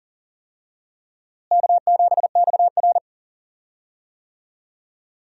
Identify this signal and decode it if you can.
Morse code, heard as a keyed tone, about 40 wpm, 700 Hz. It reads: K7XR
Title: USA Call Signs - 4 Characters - Repeated 2x - 40wpm